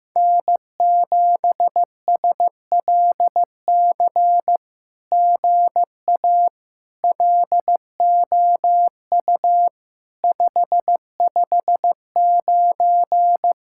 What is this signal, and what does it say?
Morse code, heard as a keyed tone, about 15 wpm, 700 Hz. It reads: N7SLC GA LOU 559
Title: SOTA QSO Practice 15wpm